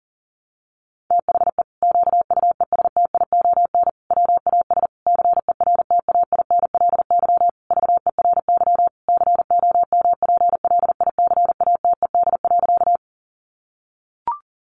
Morse code, heard as a keyed tone, about 40 wpm, 700 Hz. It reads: THE QUESTION WAS CERTAINLY VERY COMPLICATED.
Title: Sentences from Top 1200 Words 40wpm